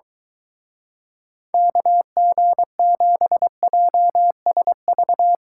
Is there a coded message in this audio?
Yes: Morse code from 1.5 to 5.3 s, heard as a keyed tone, about 23 wpm, 700 Hz. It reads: KG7JSV